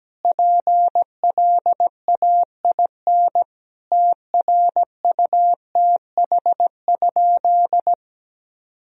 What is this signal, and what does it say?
Morse code, heard as a keyed tone, about 17 wpm, 700 Hz. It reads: PLAIN TRUTH?
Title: Sentences from Top 1000 Words - Review Part 2 - 17wpm